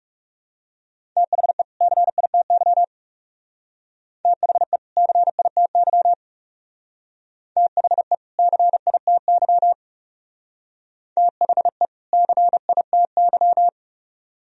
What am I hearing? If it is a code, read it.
Morse code, heard as a keyed tone, about 38 wpm, 700 Hz. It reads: THE CITY THE CITY THE CITY THE CITY